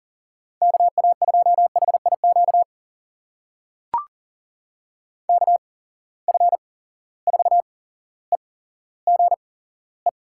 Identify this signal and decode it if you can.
Morse code, heard as a keyed tone, about 40 wpm, spaced as for 17 wpm, 700 Hz. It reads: KA2HIQ KF4EGE
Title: USA Call Signs - 8x Character Spacing - 40wpm